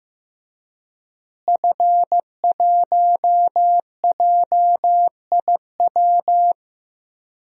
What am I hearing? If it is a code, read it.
Morse code, heard as a keyed tone, about 15 wpm, 700 Hz. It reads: F1JIW
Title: DXCC-100 Call Signs - 5 Characters - Repeated 3x - 15wpm